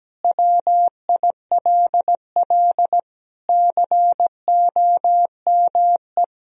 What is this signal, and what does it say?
Morse code, heard as a keyed tone, about 17 wpm, 700 Hz. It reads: WILL COME
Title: Sentences from Top 1000 Words - Review Part 1 - 17wpm